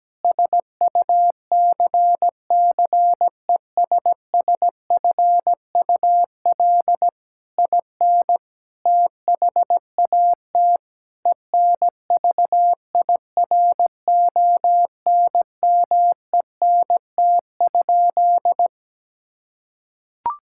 Morse code, heard as a keyed tone, about 17 wpm, 700 Hz. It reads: SUCCESSFUL IN THAT ENVIRONMENT?
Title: Sentences from Top 1000 Words - Review Part 5 - 17wpm